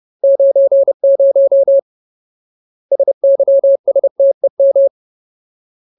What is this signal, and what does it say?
Morse code, heard as a keyed tone, about 30 wpm, 550 Hz. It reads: 90 SYSTEM
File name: Ultimate Word List / 30WPM Ultimate